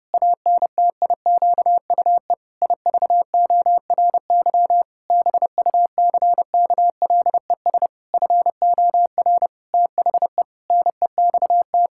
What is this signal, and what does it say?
Morse code, heard as a keyed tone, about 30 wpm, 700 Hz. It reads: ANTIQUE IVORY BUCKLES FOR THE NEXT
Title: Pangram Challenge v2 - 30wpm